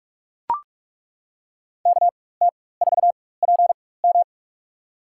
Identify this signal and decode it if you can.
Morse code, heard as a keyed tone, about 42 wpm, spaced as for 20 wpm, 700 Hz. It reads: KT4PM
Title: USA Call Signs - 4x Character Spacing - 45wpm